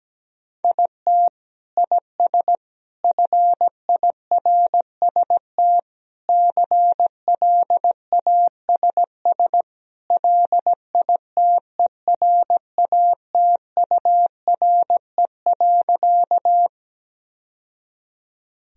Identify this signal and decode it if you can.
Morse code, heard as a keyed tone, about 17 wpm, 700 Hz. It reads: IT IS FIRST CLASS LITERATURE.
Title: Sentences from Top 1000 Words - Review Part 3 - 17wpm